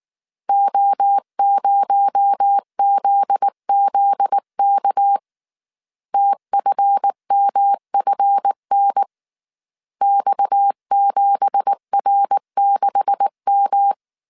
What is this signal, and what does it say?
Morse code, heard as a keyed tone, about 19 wpm, 800 Hz. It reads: O0ZZK TFMFN X7R6M
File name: CW / DET / 1 9 1500C S A 18wpm 2459